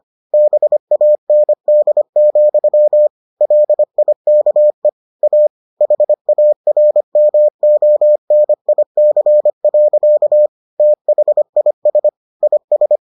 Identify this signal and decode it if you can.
Morse code, heard as a keyed tone, about 25 wpm, 600 Hz. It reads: BAND, LIKE A HARMONIC. THIS IS